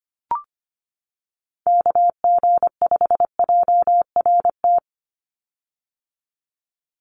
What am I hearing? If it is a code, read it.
Morse code, heard as a keyed tone, about 25 wpm, 700 Hz. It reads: KG5JRT
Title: USA Call Signs 25wpm